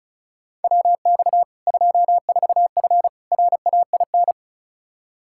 Morse code, heard as a keyed tone, about 36 wpm, 700 Hz. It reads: WX 24F RAIN
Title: K1USN SST QSO Practice 35wpm